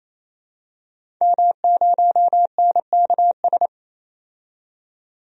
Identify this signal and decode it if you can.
Morse code, heard as a keyed tone, about 28 wpm, 700 Hz. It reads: M0NKS
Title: DXCC-100 Call Signs - 5 Characters - 28wpm